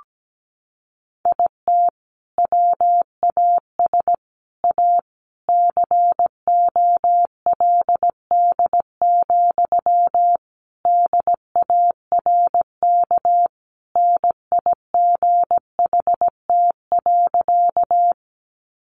Morse code, heard as a keyed tone, about 17 wpm, 700 Hz. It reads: IT WAS A COLD, DARK NIGHT.
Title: Sentences from Top 500 Words 17wpm